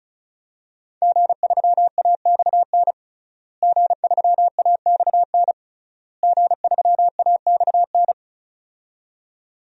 Morse code, heard as a keyed tone, about 36 wpm, 700 Hz. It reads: G3AXN G3AXN G3AXN
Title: DXCC-100 Call Signs - 5 Characters - Repeated 3x - 35wpm